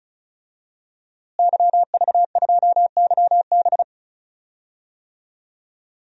Morse code, heard as a keyed tone, about 36 wpm, 700 Hz. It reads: YV2YB
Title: DXCC-100 Call Signs - 5 Characters - Repeated 3x - 35wpm